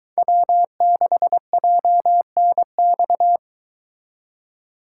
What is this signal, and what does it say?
Morse code, heard as a keyed tone, about 23 wpm, 700 Hz. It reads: W6JNX